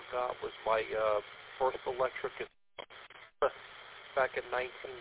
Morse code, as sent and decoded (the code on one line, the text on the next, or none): none